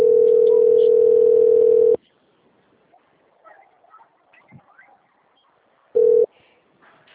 Ringback tone (North America), 440 + 480 Hz.
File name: out- - - - - .59992